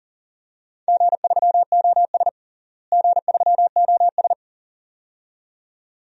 Morse code, heard as a keyed tone, about 40 wpm, 700 Hz. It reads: G3OS G3OS
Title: DXCC-100 Call Signs - 4 Characters - Repeated 2x - 40wpm